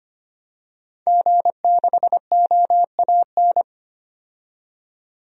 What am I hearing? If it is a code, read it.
Morse code, heard as a keyed tone, about 25 wpm, 700 Hz. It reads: G6OAN
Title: DXCC-100 Call Signs - 5 Characters - 25wpm